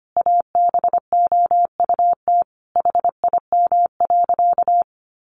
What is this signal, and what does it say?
Morse code, heard as a keyed tone, about 25 wpm, 700 Hz. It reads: ABOUT HIM.
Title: Sentences from Top 200 Words 25wpm